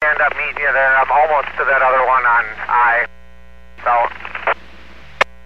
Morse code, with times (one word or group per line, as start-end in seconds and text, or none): none